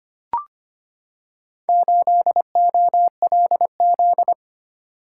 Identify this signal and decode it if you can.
Morse code, heard as a keyed tone, about 25 wpm, 700 Hz. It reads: 8OLZ
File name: practice / 25wpm / Four Letter-Number Combinations - 25wpm